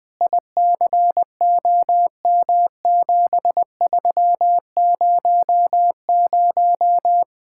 Morse code, heard as a keyed tone, about 21 wpm, 700 Hz. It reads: ICOM7300